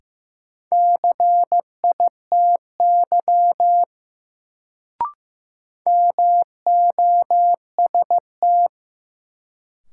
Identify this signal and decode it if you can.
Morse code, heard as a keyed tone, about 15 wpm, 700 Hz. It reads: CITY MOST
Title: Sets of 2 Words - New Word Drill - 15wpm